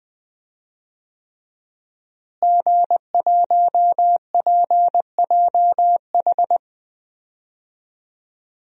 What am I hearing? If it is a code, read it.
Morse code, heard as a keyed tone, about 21 wpm, 700 Hz. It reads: G1PJH
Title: DXCC-100 Call Signs - 5 Characters - Repeated 3x - 20wpm